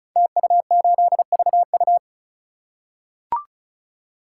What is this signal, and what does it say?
Morse code, heard as a keyed tone, about 36 wpm, 700 Hz. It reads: TU8VU
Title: Code Groups - 5 Letter:Numbers - 35wpm